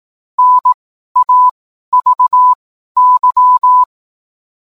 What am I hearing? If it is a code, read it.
Morse code, heard as a keyed tone, about 18 wpm, spaced as for 12 wpm, 1000 Hz. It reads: NAVY